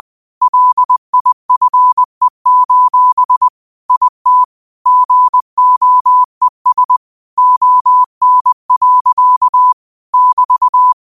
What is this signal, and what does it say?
Morse code, heard as a keyed tone, about 20 wpm, 1000 Hz. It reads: LIFE: IT GOES ON. <BT>
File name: qotdfiles / 20wpm